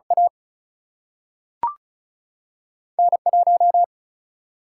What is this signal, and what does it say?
Morse code, heard as a keyed tone, about 36 wpm, 700 Hz. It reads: A N 1
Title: Single-Letter Single-Number 35wpm